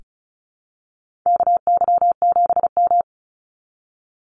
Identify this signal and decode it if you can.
Morse code, heard as a keyed tone, about 34 wpm, 700 Hz. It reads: KY7M
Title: Call Signs Repeated 3x - 35wpm